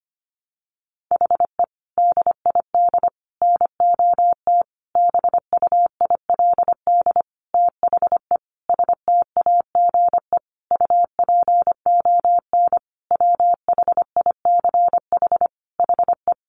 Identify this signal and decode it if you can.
Morse code, heard as a keyed tone, about 25 wpm, 700 Hz. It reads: HE DID NOT BUILD THE STAGE UPON WHICH HE
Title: Sentences from Top 1200 Words 25wpm